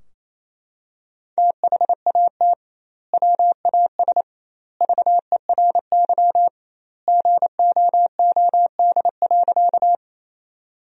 Morse code, heard as a keyed tone, about 27 wpm, 700 Hz. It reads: THAT WAS VERY GOOD.
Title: 2x Word-Spacing - Sentences from Top 200 Words - 28wpm